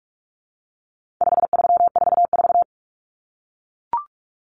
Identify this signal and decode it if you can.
Morse code, heard as a keyed tone, about 44 wpm, 700 Hz. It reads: 5344
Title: Numbers - 4-Digit 45wpm